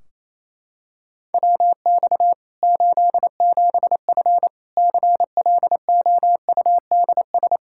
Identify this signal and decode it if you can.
Morse code, heard as a keyed tone, about 27 wpm, 700 Hz. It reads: WX 87F CLOUDS
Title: K1USN SST QSO Practice 28wpm